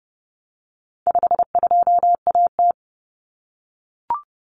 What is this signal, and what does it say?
Morse code, heard as a keyed tone, about 30 wpm, 700 Hz. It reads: 52AT
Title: Common Cut Numbers - 4-Digit 30wpm